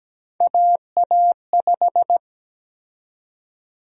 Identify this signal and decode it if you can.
Morse code, heard as a keyed tone, about 17 wpm, 700 Hz. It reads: AA5
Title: USA Call Sign Prefixes - 17wpm